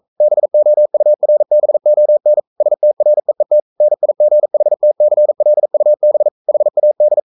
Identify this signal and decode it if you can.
Morse code, heard as a keyed tone, about 42 wpm, 600 Hz. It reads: BOURBON STREET NIGHTCLUB HAD
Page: Romanen Never af Ken Follet på engelsk.